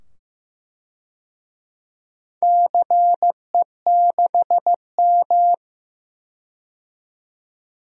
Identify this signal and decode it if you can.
Morse code, heard as a keyed tone, about 15 wpm, 700 Hz. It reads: CE6M
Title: DXCC-100 Call Signs - 4 Characters - 15wpm